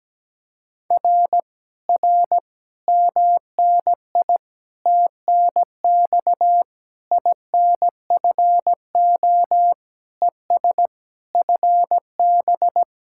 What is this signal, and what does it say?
Morse code, heard as a keyed tone, about 17 wpm, 700 Hz. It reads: R R MNI TNX INFO ES FB
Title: Speed-Racing Common QSO Phrases - 17wpm